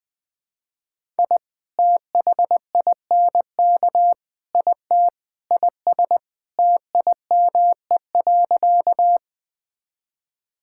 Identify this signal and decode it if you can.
Morse code, heard as a keyed tone, about 21 wpm, 700 Hz. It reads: I THINK IT IS TIME.
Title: Sentences with 4-Word Phrases - Spoken Beginning - 20wpm